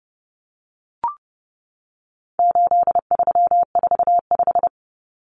Morse code, heard as a keyed tone, about 30 wpm, 700 Hz. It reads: 8345
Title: Common Cut Numbers - 4-Digit 30wpm